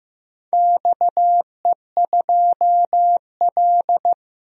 Morse code, heard as a keyed tone, about 15 wpm, 700 Hz. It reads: XE2L
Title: DXCC-100 Call Signs - 4 Characters - 15wpm